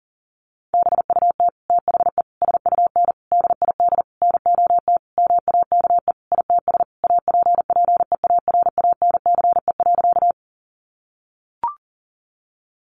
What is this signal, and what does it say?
Morse code, heard as a keyed tone, about 40 wpm, 700 Hz. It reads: BUT THE SUN DID NOT MAKE ITS APPEARANCE.
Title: Sentences from Top 1100 Words 40wpm